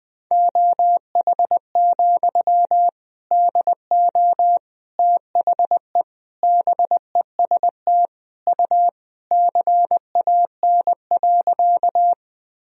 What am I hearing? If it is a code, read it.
Morse code, heard as a keyed tone, about 20 wpm, 700 Hz. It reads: OH, DO THE BEST U CAN.
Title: Sentences from Top 500 Words and Abbreviations 20wpm